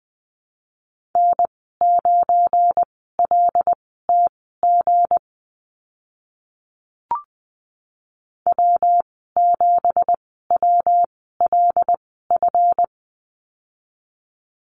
Morse code, heard as a keyed tone, about 20 wpm, spaced as for 14 wpm, 700 Hz. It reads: N9LTG W7WLF
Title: 179 Lesson 29g - Slash - Call Signs 20wpm